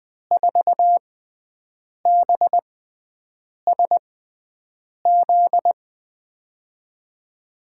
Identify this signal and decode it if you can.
Morse code, heard as a keyed tone, about 20 wpm, spaced as for 7 wpm, 700 Hz. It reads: 4BSZ
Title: USA Call Signs - 6 Characters - 6x Character Spacing - 20wpm